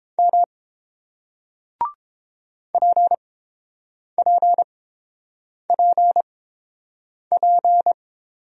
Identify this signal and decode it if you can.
Morse code, heard as a keyed tone, about 30 wpm, spaced as for 7 wpm, 700 Hz. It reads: M PPPP